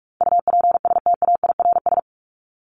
Morse code, heard as a keyed tone, about 44 wpm, 700 Hz. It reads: UPSTAIRS
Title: Top 1751-2000 Words 45wpm